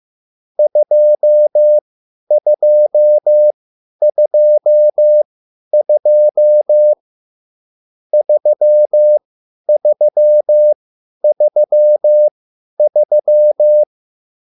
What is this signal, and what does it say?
Morse code, heard as a keyed tone, about 15 wpm, spaced as for 10 wpm, 600 Hz. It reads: 2222 3333